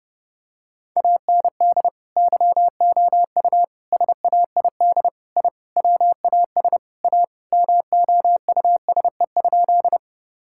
Morse code, heard as a keyed tone, about 30 wpm, 700 Hz. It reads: AND YOU SAID I WAS A MOUSE?
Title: Sentences from Top 1000 Words - Review Part 5 - 30wpm